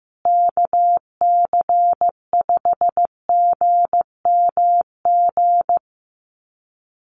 Morse code, heard as a keyed tone, about 15 wpm, 700 Hz. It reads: KC5GMG
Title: USA Call Signs Repeated 3x 15wpm